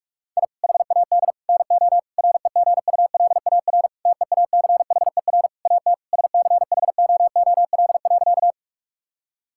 Morse code, heard as a keyed tone, about 41 wpm, 700 Hz. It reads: I HAD NO REGULAR TEACHER AT SCHOOL.
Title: Sentences from Top 1000 Words - Review Part 6 - 45wpm